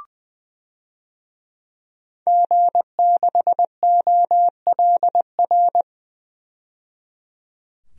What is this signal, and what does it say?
Morse code, heard as a keyed tone, about 20 wpm, 700 Hz. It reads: G6OLR
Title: DXCC-100 Call Signs - 5 Characters - 20wpm